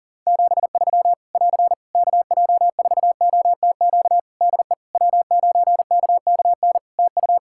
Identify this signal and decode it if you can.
Morse code, heard as a keyed tone, about 40 wpm, 700 Hz. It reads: 73 <AR> KJ4OTQ DE W9KKN TU
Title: Practice QSO - Exchange 4 - Ending - 40wpm